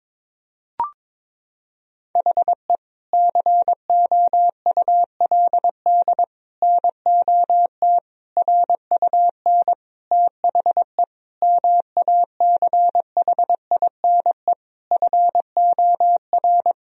Morse code, heard as a keyed tone, about 22 wpm, 700 Hz. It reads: HE COULD NOT RUN THE MACHINE FOR
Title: Sentences from Top 1000 Words - Review Part 2 - 22wpm